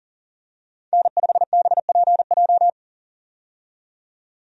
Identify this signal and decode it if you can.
Morse code, heard as a keyed tone, about 40 wpm, 700 Hz. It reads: N5BPJ
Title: USA Call Signs - 5 Characters - 40wpm